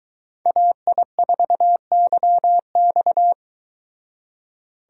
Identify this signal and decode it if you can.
Morse code, heard as a keyed tone, about 22 wpm, 700 Hz. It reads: AI4YX